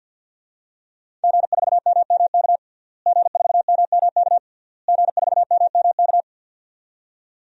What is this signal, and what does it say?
Morse code, heard as a keyed tone, about 50 wpm, 700 Hz. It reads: G4MMK G4MMK G4MMK
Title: DXCC-100 Call Signs - 5 Characters - Repeated 3x - 50wpm